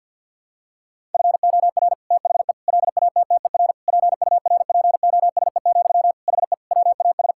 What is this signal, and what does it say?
Morse code, heard as a keyed tone, about 50 wpm, 700 Hz. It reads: FOR THE LATTER PURPOSE, HE WAS
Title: Sentences from Top 1600 Words 50wpm